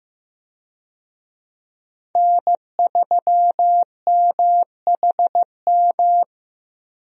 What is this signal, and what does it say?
Morse code, heard as a keyed tone, about 15 wpm, 700 Hz. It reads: N3MHM
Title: USA Call Signs - 5 Characters - Repeated 3x - 15wpm